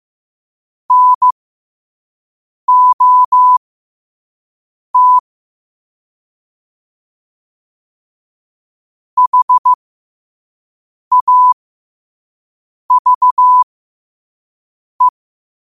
Morse code, heard as a keyed tone, about 15 wpm, spaced as for 5 wpm, 1000 Hz. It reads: NOT HAVE
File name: qotdfiles / 5wpm